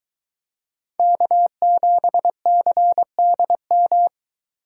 Morse code, heard as a keyed tone, about 22 wpm, 700 Hz. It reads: K7CDM